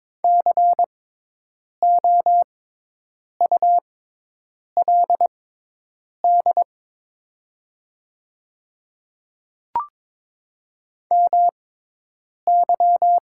Morse code, heard as a keyed tone, about 22 wpm, spaced as for 8 wpm, 700 Hz. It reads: COULD MY
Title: 6x Character Spacing - Top 300 Words - 22wpm